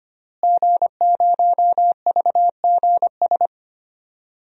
Morse code, heard as a keyed tone, about 25 wpm, 700 Hz. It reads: G0VGS
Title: DXCC-100 Call Signs - 5 Characters - Repeated 3x - 25wpm